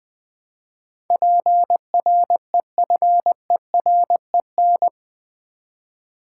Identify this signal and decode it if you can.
Morse code, heard as a keyed tone, about 20 wpm, 700 Hz. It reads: PREFEREN
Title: Build A Word - 4 Letters At A Time - 10-Letter Words - 20wpm